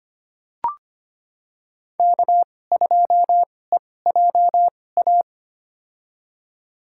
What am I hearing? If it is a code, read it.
Morse code, heard as a keyed tone, about 25 wpm, spaced as for 18 wpm, 700 Hz. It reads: K2EJA
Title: USA Call Signs - 5 Characters - 2x Character Spacing - 25wpm